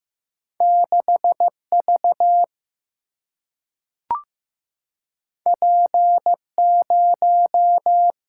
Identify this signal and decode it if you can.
Morse code, heard as a keyed tone, about 15 wpm, 700 Hz. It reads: 6V P0
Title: Single-Letter Single-Number 15wpm